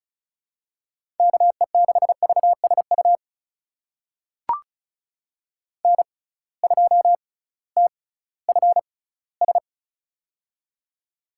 Morse code, heard as a keyed tone, about 34 wpm, spaced as for 19 wpm, 700 Hz. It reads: KE6VSU N2TFS
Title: USA Call Signs - 6x Character Spacing - 35wpm